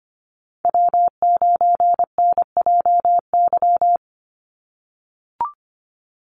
Morse code, heard as a keyed tone, about 25 wpm, 700 Hz. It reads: W9NJY
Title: Call Signs - 25wpm